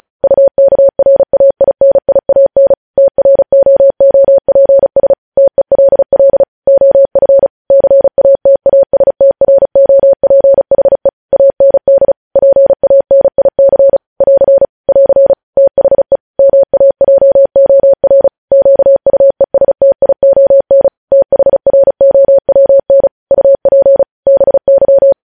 Morse code, heard as a keyed tone, about 36 wpm, 550 Hz. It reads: UKRAINIAN TROOPS TELL OF CATASTROPHE AND PANIC <AR> <AR> THE MAJOR QUESTION THROWN UP BY
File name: podcast / News headlines-1x-35x35